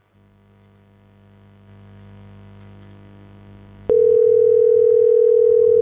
Ringback tone (North America), 440 + 480 Hz.